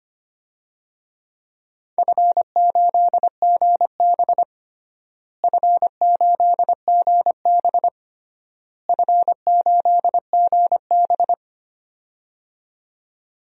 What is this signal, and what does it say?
Morse code, heard as a keyed tone, about 25 wpm, 700 Hz. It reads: F8GB F8GB F8GB
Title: DXCC-100 Call Signs - 4 Characters - Repeated 3x - 25wpm